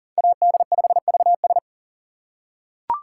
Morse code, heard as a keyed tone, about 37 wpm, 700 Hz. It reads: AD5VS
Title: USA Call Signs - 5 Characters - 6x Character Spacing - 40wpm